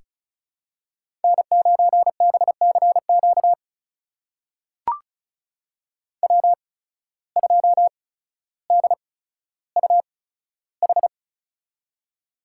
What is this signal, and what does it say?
Morse code, heard as a keyed tone, about 36 wpm, spaced as for 15 wpm, 700 Hz. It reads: N9BCQ W2DUH
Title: USA Call Signs - 5 Characters - 8x Character Spacing - 35wpm